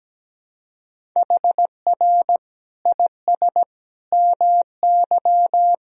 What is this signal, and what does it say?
Morse code, heard as a keyed tone, about 17 wpm, 700 Hz. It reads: HR IS MY
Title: Sentences from Top 200 Words and Abbreviations 17wpm